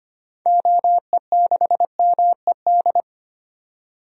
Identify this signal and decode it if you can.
Morse code, heard as a keyed tone, about 25 wpm, 700 Hz. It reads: OE6MED
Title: DXCC-100 Call Signs - 6 Characters - 25wpm